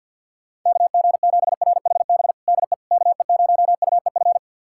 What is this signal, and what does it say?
Morse code, heard as a keyed tone, about 51 wpm, 700 Hz. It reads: KG7RHB DE KE0UEF
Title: USA Callsign DE Callsign - 6 Characters - 50wpm